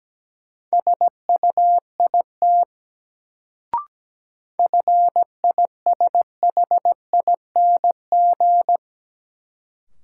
Morse code, heard as a keyed tone, about 17 wpm, 700 Hz. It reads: SUIT FISHING
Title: Sentences from Top 1600 Words - New Word Drill Alt - 17wpm